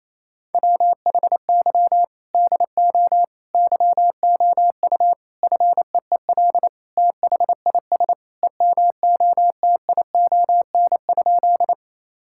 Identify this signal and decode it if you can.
Morse code, heard as a keyed tone, about 28 wpm, 700 Hz. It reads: WHY DO YOU FEEL THIS EMOTION?
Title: Sentences from Top 1600 Words 28wpm